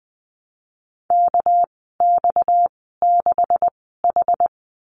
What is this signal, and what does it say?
Morse code, heard as a keyed tone, about 20 wpm, spaced as for 14 wpm, 700 Hz. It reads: KX6H
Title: 238 Lesson 38g - Letter q - Call Signs 20wpm